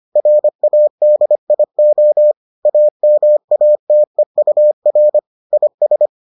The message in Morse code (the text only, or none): RADIO AMATEUR IS